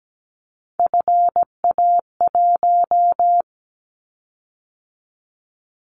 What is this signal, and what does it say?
Morse code, heard as a keyed tone, about 17 wpm, 700 Hz. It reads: FA1
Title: Callsign Prefixes for Top 25 DXCC - 17wpm